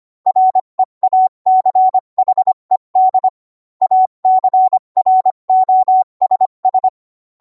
Morse code, heard as a keyed tone, about 25 wpm, 750 Hz. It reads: REACHED ACROSS